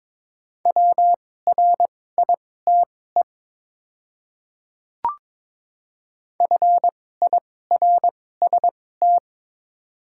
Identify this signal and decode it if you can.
Morse code, heard as a keyed tone, about 22 wpm, spaced as for 16 wpm, 700 Hz. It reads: WRITE FIRST
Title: 2x Character Spacing - Top 100 Words - 22wpm